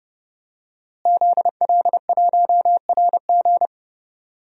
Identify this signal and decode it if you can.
Morse code, heard as a keyed tone, about 31 wpm, 700 Hz. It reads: ZL1RG
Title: DXCC-100 Call Signs - 5 Characters - Repeated 2x - 30wpm